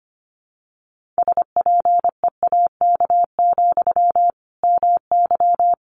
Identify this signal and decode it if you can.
Morse code, heard as a keyed tone, about 25 wpm, 700 Hz. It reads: SPEAK, MY
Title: Sentences from Top 1500 Words 25wpm